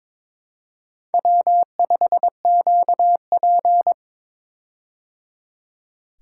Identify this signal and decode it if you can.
Morse code, heard as a keyed tone, about 22 wpm, 700 Hz. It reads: W5QP
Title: K1USN SST QSO Practice - 22wpm